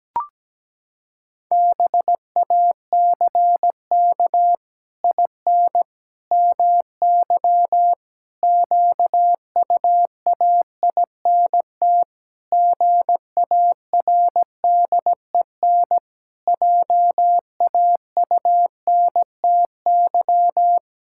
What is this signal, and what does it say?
Morse code, heard as a keyed tone, about 17 wpm, 700 Hz. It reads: BACK IN MY QUAINT GARDEN JAUNTY
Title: Pangram Challenge v2 - 17wpm